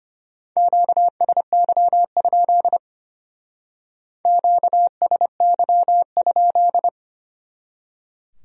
Morse code, heard as a keyed tone, about 30 wpm, 700 Hz. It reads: QSY? QSY?
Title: Speed-Racing Q-Codes - 25wpm